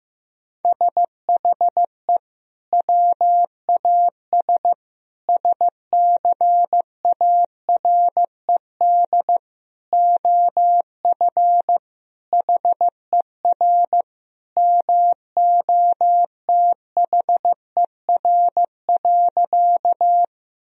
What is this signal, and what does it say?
Morse code, heard as a keyed tone, about 15 wpm, 700 Hz. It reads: SHE WAS SCARED OF HER MOTHER.
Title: Sentences from Top 1000 Words - Review Part 4 - 15wpm